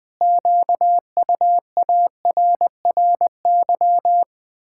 Morse code, heard as a keyed tone, about 21 wpm, 700 Hz. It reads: QUARRY
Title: Pangram Challenge v2 - 20wpm